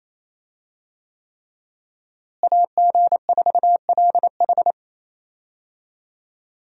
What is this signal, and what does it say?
Morse code, heard as a keyed tone, about 28 wpm, 700 Hz. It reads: AG4LH
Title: USA Call Signs - 5 Characters - Repeated 3x - 28wpm